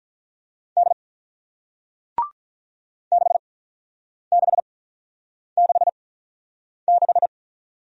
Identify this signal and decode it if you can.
Morse code, heard as a keyed tone, about 48 wpm, spaced as for 9 wpm, 700 Hz. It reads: D 6666